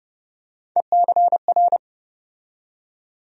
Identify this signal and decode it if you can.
Morse code, heard as a keyed tone, about 30 wpm, 700 Hz. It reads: ECR
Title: USA Call Sign Postfixes - 3 Letters 30wpm